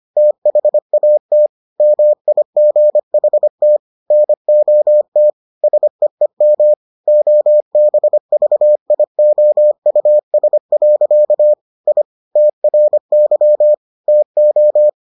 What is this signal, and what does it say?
Morse code, heard as a keyed tone, about 25 wpm, 600 Hz. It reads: THAT MIGHT NOT SEEM OBVIOUS. I TRY TO